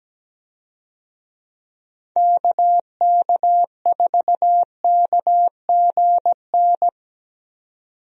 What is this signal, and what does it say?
Morse code, heard as a keyed tone, about 17 wpm, 700 Hz. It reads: KK4KGN
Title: USA Call Signs - 6 Characters - 17wpm